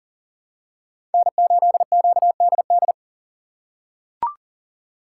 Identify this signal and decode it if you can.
Morse code, heard as a keyed tone, about 40 wpm, 700 Hz. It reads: N8QDD
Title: USA Call Signs - 5 Characters - 6x Character Spacing - 40wpm